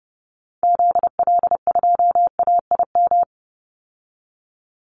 Morse code, heard as a keyed tone, about 30 wpm, 700 Hz. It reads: ZL2AIM
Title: Call Signs Repeated 3x - 30wpm